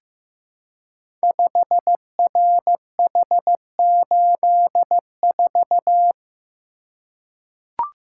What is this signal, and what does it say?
Morse code, heard as a keyed tone, about 15 wpm, 700 Hz. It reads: 5RH84
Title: Code Groups - 5 Letter:Numbers - 15wpm